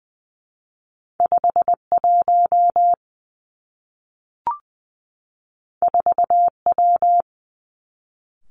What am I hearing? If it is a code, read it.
Morse code, heard as a keyed tone, about 20 wpm, 700 Hz. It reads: 5 1 4 W
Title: Two Challenging Letter-Number Combinations 20wpm